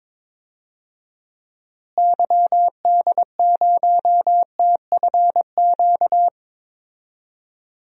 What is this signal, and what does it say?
Morse code, heard as a keyed tone, about 21 wpm, 700 Hz. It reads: YD0TFQ
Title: DXCC-100 Call Signs - 6 Characters - Repeated 3x - 22wpm